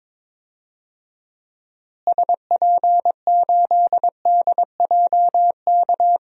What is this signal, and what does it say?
Morse code, heard as a keyed tone, about 21 wpm, 700 Hz. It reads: SP8DJK
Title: DXCC-100 Call Signs - 6 Characters - Repeated 3x - 22wpm